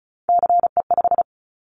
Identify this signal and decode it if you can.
Morse code, heard as a keyed tone, about 36 wpm, 700 Hz. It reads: CE5
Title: Callsign Prefixes for Top 100 DXCC - 35wpm